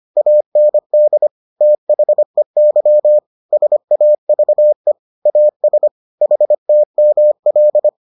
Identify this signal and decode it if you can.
Morse code, heard as a keyed tone, about 25 wpm, 600 Hz. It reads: AND THEY SAVE AS HTML